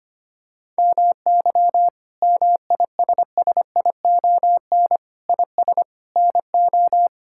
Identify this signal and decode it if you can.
Morse code, heard as a keyed tone, about 25 wpm, 700 Hz. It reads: MY MISSION IS NO
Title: Sentences from Top 1600 Words 25wpm